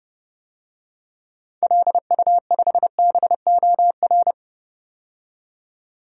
Morse code, heard as a keyed tone, about 30 wpm, 700 Hz. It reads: LU5BOR
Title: DXCC-100 Call Signs - 6 Characters - 30wpm